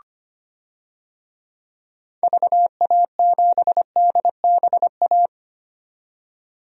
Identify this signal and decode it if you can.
Morse code, heard as a keyed tone, about 25 wpm, 700 Hz. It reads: VA7DBA
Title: DXCC-100 Call Signs - 6 Characters - 25wpm